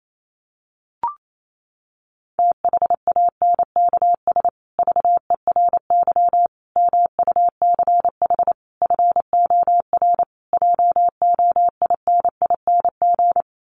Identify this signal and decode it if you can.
Morse code, heard as a keyed tone, about 28 wpm, 700 Hz. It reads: THANKS VERY MUCH FOR JOINING
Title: Sets of 5 Words Encore2 28wpm